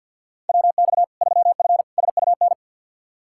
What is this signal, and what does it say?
Morse code, heard as a keyed tone, about 52 wpm, 700 Hz. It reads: WX 3F SUN
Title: Practice QSO - Exchange 2 - Info - 50wpm